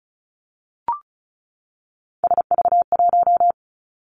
Morse code, heard as a keyed tone, about 36 wpm, 700 Hz. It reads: SV1
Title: Callsign Prefixes for Top 50 DXCC - 35wpm